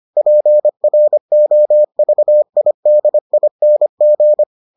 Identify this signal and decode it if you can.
Morse code, heard as a keyed tone, about 25 wpm, 600 Hz. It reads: PROVIDING